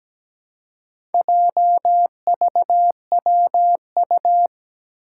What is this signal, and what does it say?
Morse code, heard as a keyed tone, about 17 wpm, 700 Hz. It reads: JVWU
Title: Four Challenging Letter-Number Combinations 17wpm